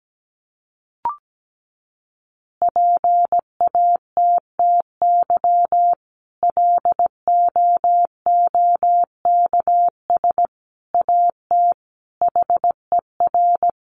Morse code, heard as a keyed tone, about 17 wpm, 700 Hz. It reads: PATTY LOOKS AT HER
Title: Additional Easy Phrases 17wpm